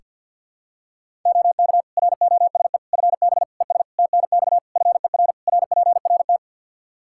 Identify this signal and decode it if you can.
Morse code, heard as a keyed tone, about 47 wpm, 700 Hz. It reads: OK ROSE FB ES TNX FER RPRT
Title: Practice QSO - Exchange 2 - Info - 50wpm